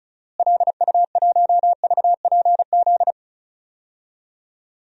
Morse code, heard as a keyed tone, about 34 wpm, 700 Hz. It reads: LU1VPZ
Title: DXCC-100 Call Signs - 6 Characters - Repeated 2x - 35wpm